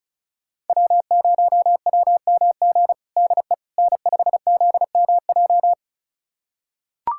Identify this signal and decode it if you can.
Morse code, heard as a keyed tone, about 36 wpm, 700 Hz. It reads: W0WMG DE N5ZMJ
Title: USA Callsign DE Callsign - 5 Characters - 35wpm